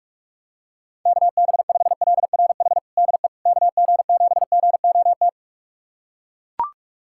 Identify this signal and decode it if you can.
Morse code, heard as a keyed tone, about 42 wpm, 700 Hz. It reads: KB5LRH DE KG7GOT
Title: USA Callsign DE Callsign 45wpm